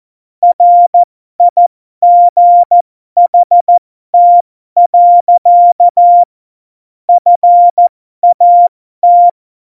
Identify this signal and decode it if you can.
Morse code, heard as a keyed tone, about 14 wpm, spaced as for 12 wpm, 700 Hz. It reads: RIGHT. FAT
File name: Morsekurs / Witze